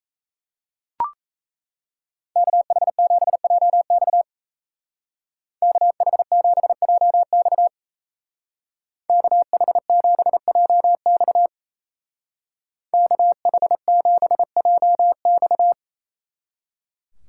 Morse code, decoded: KH7JX KH7JX KH7JX KH7JX

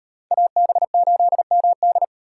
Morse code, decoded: AB8MD

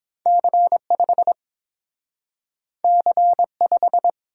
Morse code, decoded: C5 C5

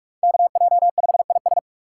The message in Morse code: KJ5IS

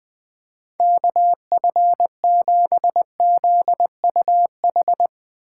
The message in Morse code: KF7ZUH

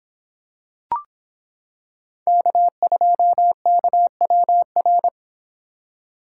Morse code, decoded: K2KWR

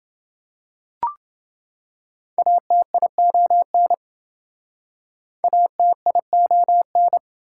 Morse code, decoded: ATION ATION

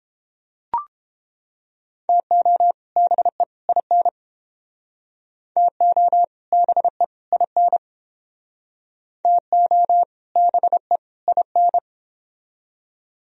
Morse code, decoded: TO BE IN TO BE IN TO BE IN

